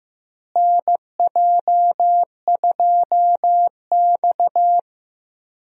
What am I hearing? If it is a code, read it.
Morse code, heard as a keyed tone, about 15 wpm, 700 Hz. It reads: NJ2X
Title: USA Call Signs - 4 Characters - Repeated 3x - 15wpm